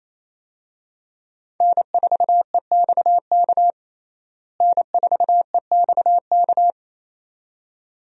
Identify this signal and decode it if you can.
Morse code, heard as a keyed tone, about 28 wpm, 700 Hz. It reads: N4EXK N4EXK
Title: USA Call Signs - 5 Characters - Repeated 2x - 28wpm